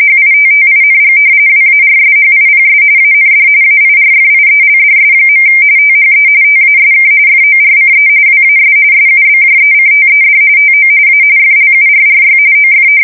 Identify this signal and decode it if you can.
RTTY, about 45.45 baud. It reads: HITTING WARSHIPS AND AIRFIELDS. THE TASK FORCE WAS SUBJECTED TO A FLURRY OF
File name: fsk170 / battleshipnj